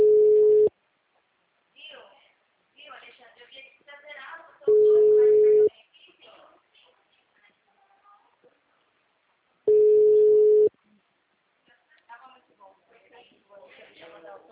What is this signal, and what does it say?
Ringback tone (Europe, 425 Hz), with speech.